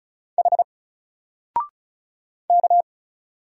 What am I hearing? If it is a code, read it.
Morse code, heard as a keyed tone, about 35 wpm, 700 Hz. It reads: H K
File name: Single-Letter-Number-35wpm